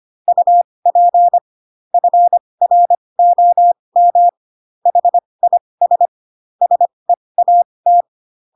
Morse code, heard as a keyed tone, about 25 wpm, spaced as for 20 wpm, 700 Hz. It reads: UP FROM HIS SEAT
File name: Pinocchio / Pinocchio ch7 15 25 ik1hge